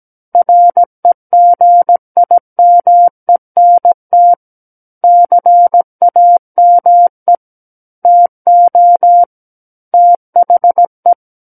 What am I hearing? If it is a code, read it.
Morse code, heard as a keyed tone, about 17 wpm, 700 Hz. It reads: REGIMENT CAME TO THE
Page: SKCC Learning Center - Morse Code Practice Files